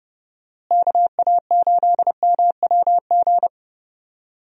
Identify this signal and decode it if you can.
Morse code, heard as a keyed tone, about 30 wpm, 700 Hz. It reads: KA8MWG
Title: USA Call Signs - 6 Characters - Repeated 2x - 30wpm